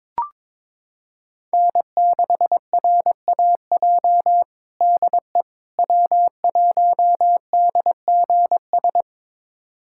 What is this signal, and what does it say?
Morse code, heard as a keyed tone, about 22 wpm, 700 Hz. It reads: N6RAJ DE W1DGS
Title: USA Callsign DE Callsign - 5 Characters - 22wpm